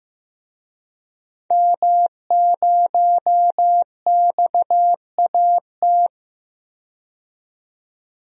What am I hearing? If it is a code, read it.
Morse code, heard as a keyed tone, about 15 wpm, 700 Hz. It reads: M0XAT
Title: DXCC-100 Call Signs - 5 Characters - 15wpm